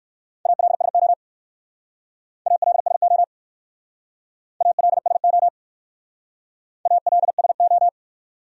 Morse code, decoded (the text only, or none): ALSO ALSO ALSO ALSO